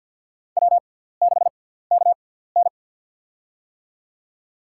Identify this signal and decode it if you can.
Morse code, heard as a keyed tone, about 52 wpm, spaced as for 17 wpm, 700 Hz. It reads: W6KN
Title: USA Call Signs - 4 Characters - 6x Character Spacing - 50wpm